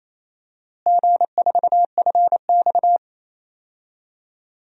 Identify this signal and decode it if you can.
Morse code, heard as a keyed tone, about 27 wpm, 700 Hz. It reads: G4FX
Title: DXCC-100 Call Signs - 4 Characters - Repeated 2x - 28wpm